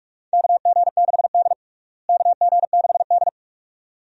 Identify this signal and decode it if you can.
Morse code, heard as a keyed tone, about 41 wpm, 700 Hz. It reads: KG6D KG6D
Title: USA Call Signs - 4 Characters - Repeated 2x - 45wpm